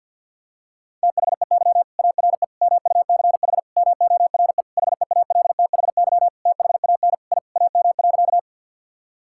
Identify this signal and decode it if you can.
Morse code, heard as a keyed tone, about 52 wpm, 700 Hz. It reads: THEY ARE MUCH MORE HEALTHY THAN I AM.
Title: Sentences from Top 1000 Words - Review Part 5 - 50wpm